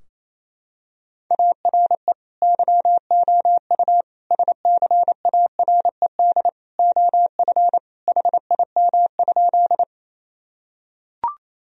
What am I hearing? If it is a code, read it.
Morse code, heard as a keyed tone, about 27 wpm, 700 Hz. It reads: ARE YOU SCARED OF HIM?
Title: Sentences from Top 1000 Words - Review Part 4 - 28wpm